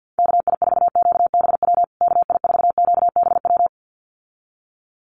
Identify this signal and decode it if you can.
Morse code, heard as a keyed tone, about 50 wpm, 700 Hz. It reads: KI4QBW KI4QBW
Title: USA Call Signs Repeated 2x 50wpm